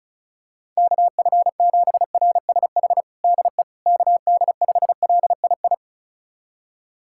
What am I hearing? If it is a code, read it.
Morse code, heard as a keyed tone, about 34 wpm, 700 Hz. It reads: KF7RSH DE KD5LII
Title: USA Callsign DE Callsign 35wpm